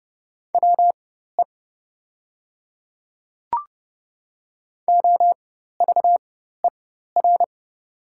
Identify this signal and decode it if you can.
Morse code, heard as a keyed tone, about 30 wpm, spaced as for 14 wpm, 700 Hz. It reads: WE OVER
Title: 4x Character Spacing - Top 200 Words - 30wpm